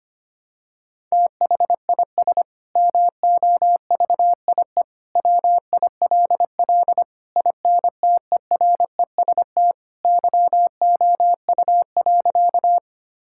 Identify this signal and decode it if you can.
Morse code, heard as a keyed tone, about 25 wpm, 700 Hz. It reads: THIS MOVIE WILL INTEREST YOU.
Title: Sentences from Top 1000 Words - Review Part 5 - 25wpm